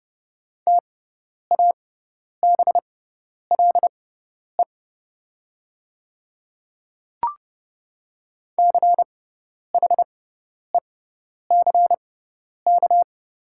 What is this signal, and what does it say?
Morse code, heard as a keyed tone, about 30 wpm, spaced as for 10 wpm, 700 Hz. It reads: TABLE CHECK
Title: 6x Character Spacing - Top 500 Words - 30wpm